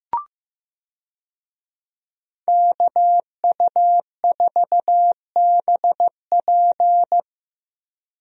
Morse code, heard as a keyed tone, about 15 wpm, 700 Hz. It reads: KU4BP
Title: USA Call Signs - 5 Characters - Repeated 2x - 15wpm